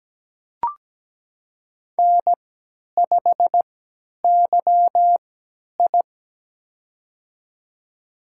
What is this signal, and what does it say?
Morse code, heard as a keyed tone, about 17 wpm, spaced as for 10 wpm, 700 Hz. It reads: N5YI
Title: USA Call Signs - 4 Characters - 3x Character Spacing - 17wpm